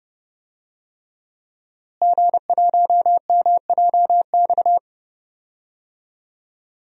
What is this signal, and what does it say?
Morse code, heard as a keyed tone, about 30 wpm, 700 Hz. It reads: G1MJX
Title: DXCC-100 Call Signs - 5 Characters - Repeated 3x - 30wpm